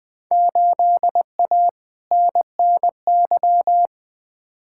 Morse code, heard as a keyed tone, about 20 wpm, 700 Hz. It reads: 8A NNY
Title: Field Day Exchanges Encore - 20wpm